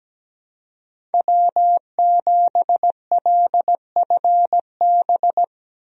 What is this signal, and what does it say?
Morse code, heard as a keyed tone, about 17 wpm, 700 Hz. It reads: W7LFB
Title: USA Call Signs - 5 Characters - Repeated 3x - 17wpm